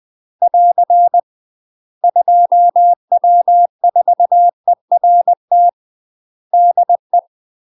Morse code, heard as a keyed tone, about 20 wpm, 700 Hz. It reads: <AR> 2W4ERT DE
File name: UK MASTER QSOs TRAINING / 20WPM / UK MASTER QSOs 0003